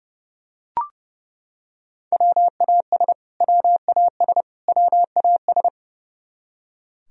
Morse code, heard as a keyed tone, about 30 wpm, 700 Hz. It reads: WAS WAS WAS
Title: Top 100 Words Repeated 30wpm